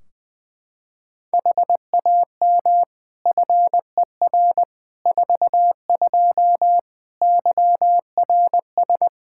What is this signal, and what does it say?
Morse code, heard as a keyed tone, about 21 wpm, 700 Hz. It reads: HAM FER 42 YRS
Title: Practice QSO - Exchange 3 - More Info - 20wpm